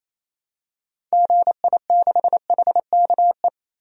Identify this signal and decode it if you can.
Morse code, heard as a keyed tone, about 28 wpm, 700 Hz. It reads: GI6HKE
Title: DXCC-100 Call Signs - 6 Characters - 28wpm